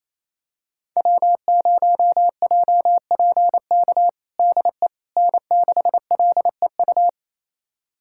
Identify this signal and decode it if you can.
Morse code, heard as a keyed tone, about 28 wpm, 700 Hz. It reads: W0JPK DE N6LEU
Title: USA Callsign DE Callsign - 5 Characters - 28wpm